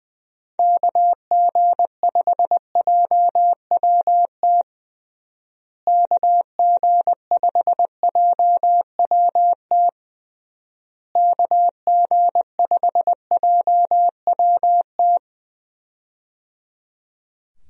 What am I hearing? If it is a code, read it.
Morse code, heard as a keyed tone, about 20 wpm, 700 Hz. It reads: KG5JWT KG5JWT KG5JWT
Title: USA Call Signs - 6 Characters - Repeated 3x - 20wpm